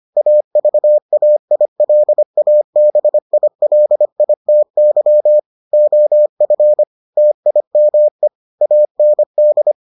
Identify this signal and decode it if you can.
Morse code, heard as a keyed tone, about 25 wpm, 600 Hz. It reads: AVAILABILITY OF TIME AND